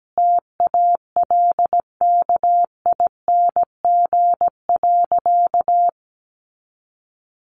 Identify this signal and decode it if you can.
Morse code, heard as a keyed tone, about 17 wpm, 700 Hz. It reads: TALKING.
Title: Additional Easy Phrases 17wpm